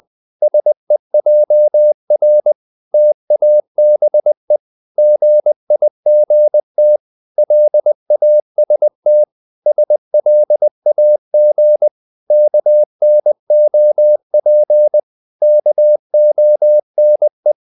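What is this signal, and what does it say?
Morse code, heard as a keyed tone, about 20 wpm, 600 Hz. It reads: SEJR TABE GIGT LAST SLAG KNOP KONE
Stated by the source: Morse code practice of short Danish words at 20 wpm, from CW med Gnister.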